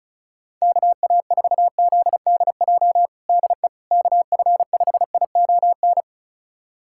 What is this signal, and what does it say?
Morse code, heard as a keyed tone, about 36 wpm, 700 Hz. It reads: KA4ZDJ DE KF5ION
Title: USA Callsign DE Callsign - 6 Characters - 35wpm